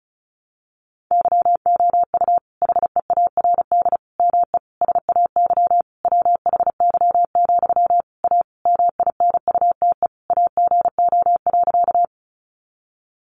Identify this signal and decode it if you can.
Morse code, heard as a keyed tone, about 34 wpm, 700 Hz. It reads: YOU HEARD ME SAY WHY, A MINUTE AGO.
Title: Sentences from Top 500 Words 35wpm